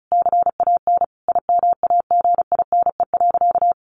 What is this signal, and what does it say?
Morse code, heard as a keyed tone, about 34 wpm, 700 Hz. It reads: CAN IMAGINE.
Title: Sentences from Top 1500 Words 35wpm